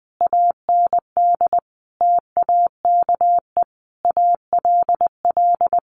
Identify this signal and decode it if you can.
Morse code, heard as a keyed tone, about 19 wpm, 700 Hz. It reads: AND TAKE ALL
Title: Sentences from Top 900 Words 20wpm